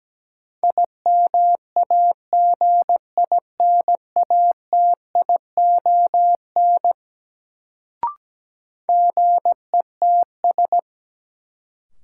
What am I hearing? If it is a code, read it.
Morse code, heard as a keyed tone, about 17 wpm, 700 Hz. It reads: IMAGINATION GETS
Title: Sentences from Top 1300 Words - New Word Drill - 17wpm